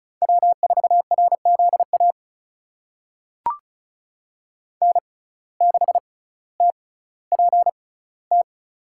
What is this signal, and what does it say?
Morse code, heard as a keyed tone, about 36 wpm, spaced as for 18 wpm, 700 Hz. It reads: W4RZA N6TPT
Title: USA Call Signs - 5 Characters - 6x Character Spacing - 35wpm